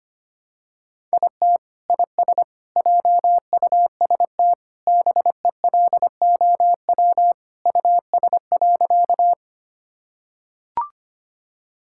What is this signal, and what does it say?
Morse code, heard as a keyed tone, about 25 wpm, 700 Hz. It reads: IT IS JUST BELOW US.
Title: Sentences from Top 1000 Words - Review Part 5 - 25wpm